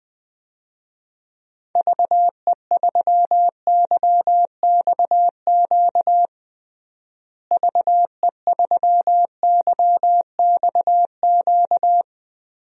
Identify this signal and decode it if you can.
Morse code, heard as a keyed tone, about 20 wpm, 700 Hz. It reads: VE3YXQ VE3YXQ
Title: DXCC-100 Call Signs - 6 Characters - Repeated 3x - 20wpm